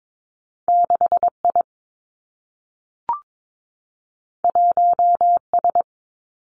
Morse code, heard as a keyed tone, about 22 wpm, 700 Hz. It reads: 6 I 1 S
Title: Two Challenging Letter-Number Combinations 22wpm